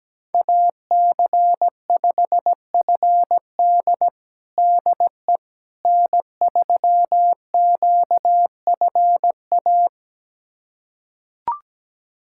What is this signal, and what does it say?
Morse code, heard as a keyed tone, about 17 wpm, 700 Hz. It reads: AC5FD DE N3QFA
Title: USA Callsign DE Callsign - 5 Characters - 17wpm